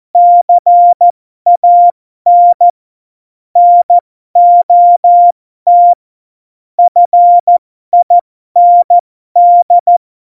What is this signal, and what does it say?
Morse code, heard as a keyed tone, about 14 wpm, spaced as for 12 wpm, 700 Hz. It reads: CAN NOT FIND
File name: Morsekurs / Witze